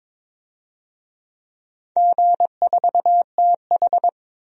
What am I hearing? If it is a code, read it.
Morse code, heard as a keyed tone, about 22 wpm, 700 Hz. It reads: G4TH
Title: DXCC-100 Call Signs - 4 Characters - 22wpm